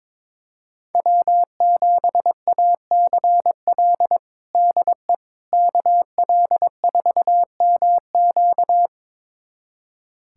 Morse code, heard as a keyed tone, about 21 wpm, 700 Hz. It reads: W7ACL DE KL4MQ
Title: USA Callsign DE Callsign - 5 Characters - 22wpm